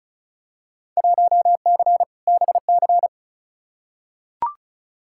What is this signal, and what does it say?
Morse code, heard as a keyed tone, about 36 wpm, 700 Hz. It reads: 1C BC
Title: Field Day Exchanges Encore - 35wpm